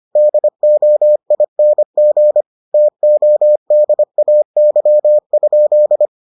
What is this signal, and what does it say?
Morse code, heard as a keyed tone, about 25 wpm, 600 Hz. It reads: DOING TODAY?